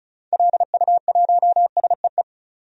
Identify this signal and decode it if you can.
Morse code, heard as a keyed tone, about 34 wpm, 700 Hz. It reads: LU1SEE
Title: DXCC-100 Call Signs - 6 Characters - 35wpm